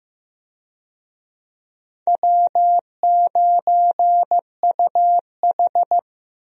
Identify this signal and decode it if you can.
Morse code, heard as a keyed tone, about 15 wpm, 700 Hz. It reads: W9UH
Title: USA Call Signs - 4 Characters - Repeated 2x - 15wpm